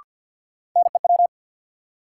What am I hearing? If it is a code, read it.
Morse code, heard as a keyed tone, about 53 wpm, 700 Hz. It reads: NEW
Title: Sets of 2 Words - New Word Drill - 50wpm